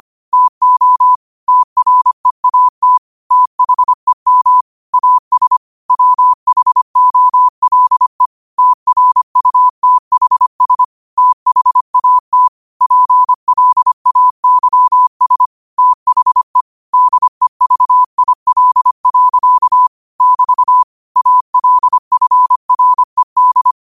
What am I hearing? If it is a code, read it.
Morse code, heard as a keyed tone, about 25 wpm, 1000 Hz. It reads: TO TREAT THEM AS WHOLE TRUTHS THAT PLAYS THE DEVIL. <BT> ALFRED
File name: qotdfiles / 25wpm